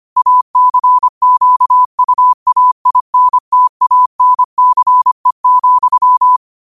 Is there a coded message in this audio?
Yes: Morse code from 0.2 to 6.4 s, heard as a keyed tone, about 25 wpm, 1000 Hz. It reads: ACQUAINTANCE,